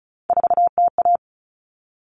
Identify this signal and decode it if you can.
Morse code, heard as a keyed tone, about 34 wpm, 700 Hz. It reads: 4TA
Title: All Cut Numbers - 3-Digit 35wpm